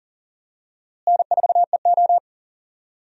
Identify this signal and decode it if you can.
Morse code, heard as a keyed tone, about 37 wpm, 700 Hz. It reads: N4EO
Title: USA Call Signs - 4 Characters - Repeated 2x - 40wpm